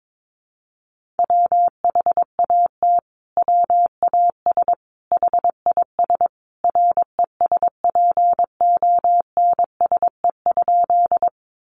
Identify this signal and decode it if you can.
Morse code, heard as a keyed tone, about 22 wpm, 700 Hz. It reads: WHAT WAS HIS RESPONSE?
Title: Sentences from Top 1400 Words 22wpm